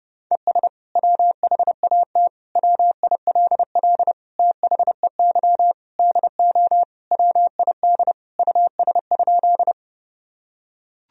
Morse code, heard as a keyed tone, about 32 wpm, 700 Hz. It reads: ES WHAT WILL THEY DO WID US?
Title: Sentences from Top 200 Words and Abbreviations 30wpm